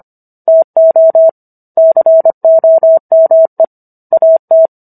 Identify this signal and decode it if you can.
Morse code, heard as a keyed tone, about 25 wpm, 650 Hz. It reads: TO COME AT